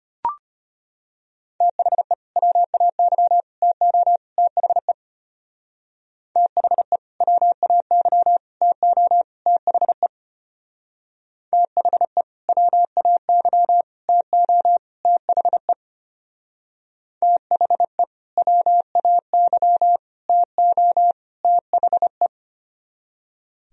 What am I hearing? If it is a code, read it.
Morse code, heard as a keyed tone, about 32 wpm, 690 Hz. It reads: THE WAY TO THE THE WAY TO THE THE WAY TO THE THE WAY TO THE